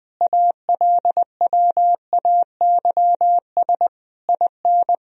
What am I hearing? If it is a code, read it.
Morse code, heard as a keyed tone, about 21 wpm, 700 Hz. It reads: ALWAYS IN
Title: Sentences from Top 1000 Words - Review Part 3 - 20wpm